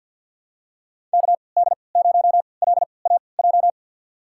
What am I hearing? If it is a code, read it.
Morse code, heard as a keyed tone, about 51 wpm, spaced as for 28 wpm, 700 Hz. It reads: KD0LAJ
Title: USA Call Signs - 3x Character Spacing - 50wpm